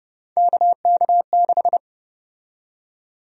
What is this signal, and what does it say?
Morse code, heard as a keyed tone, about 30 wpm, 700 Hz. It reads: KK6
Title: USA Call Sign Prefixes - 30wpm